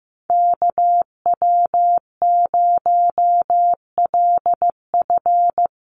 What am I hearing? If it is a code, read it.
Morse code, heard as a keyed tone, about 15 wpm, 700 Hz. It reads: KW0LF
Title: USA Call Signs Repeated 3x 15wpm